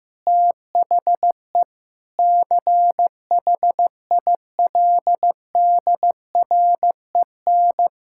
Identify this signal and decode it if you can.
Morse code, heard as a keyed tone, about 15 wpm, 700 Hz. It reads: THE CHILDREN
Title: Sentences from Top 1000 Words - Review Part 6 - 15wpm